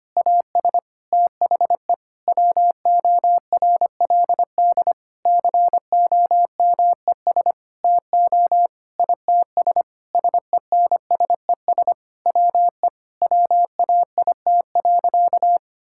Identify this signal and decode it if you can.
Morse code, heard as a keyed tone, about 25 wpm, 700 Hz. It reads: AS THE WORLD COMES TO ITS SENSES WE WAIT.
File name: practice / 25wpm / Sentences with 9-Word Phrases - 25wpm